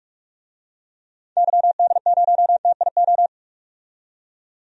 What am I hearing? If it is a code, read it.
Morse code, heard as a keyed tone, about 46 wpm, 700 Hz. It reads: YD0TIO
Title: DXCC-100 Call Signs - 6 Characters - 45wpm